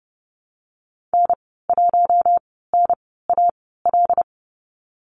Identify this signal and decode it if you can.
Morse code, heard as a keyed tone, about 31 wpm, 700 Hz. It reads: N 1 N A L
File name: Lesson 19g - Letter w - Call Signs 30wpm